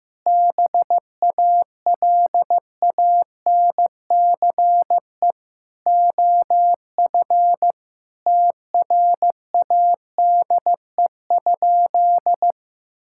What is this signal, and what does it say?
Morse code, heard as a keyed tone, about 16 wpm, 700 Hz. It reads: BALANCE OF TRADE?
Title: Sentences from Top 1000 Words - Review Part 6 - 15wpm